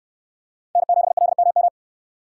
Morse code, heard as a keyed tone, about 64 wpm, 700 Hz. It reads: N7LMM